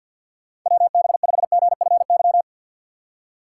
Morse code, heard as a keyed tone, about 53 wpm, 700 Hz. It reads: W B 5 G F Y
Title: USA Call Signs - 6 Characters - 6x Character Spacing - 50wpm